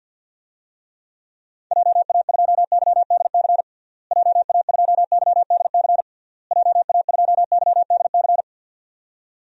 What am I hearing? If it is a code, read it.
Morse code, heard as a keyed tone, about 52 wpm, 700 Hz. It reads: JA2YDC JA2YDC JA2YDC
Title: DXCC-100 Call Signs - 6 Characters - Repeated 3x - 50wpm